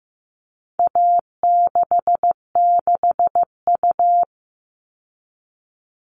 Morse code, heard as a keyed tone, about 15 wpm, 700 Hz. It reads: A66U
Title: All Cut Numbers - 4-Digit 15wpm